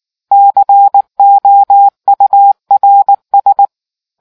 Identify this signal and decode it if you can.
Morse code, heard as a keyed tone, about 19 wpm, 800 Hz. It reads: COURS